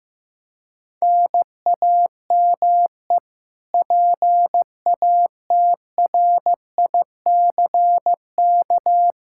Morse code, heard as a keyed tone, about 15 wpm, 700 Hz. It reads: NAME PATRICK
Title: Practice QSO - Exchange 1 - Beginning - 15wpm